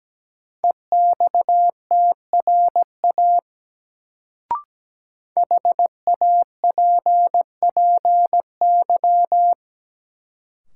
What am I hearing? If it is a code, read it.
Morse code, heard as a keyed tone, about 17 wpm, 700 Hz. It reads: EXTRA HAPPY
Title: Sentences from Top 1000 Words - New Word Drill - 17wpm